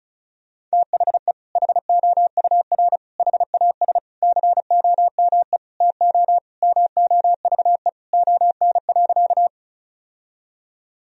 Morse code, heard as a keyed tone, about 34 wpm, 700 Hz. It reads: THE HOUR HAS COME TO MOVE ON.
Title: Sentences from Top 1000 Words - Review Part 1 - 35wpm